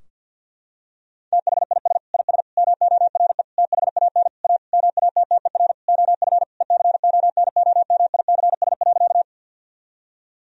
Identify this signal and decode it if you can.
Morse code, heard as a keyed tone, about 50 wpm, 700 Hz. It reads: THIS IS MORE THAN A MATTER OF ECONOMICS.
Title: Sentences from Top 1000 Words - Review Part 3 - 50wpm